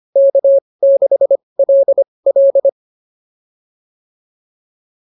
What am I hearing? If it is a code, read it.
Morse code, heard as a keyed tone, about 25 wpm, spaced as for 20 wpm, 560 Hz. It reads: K6LL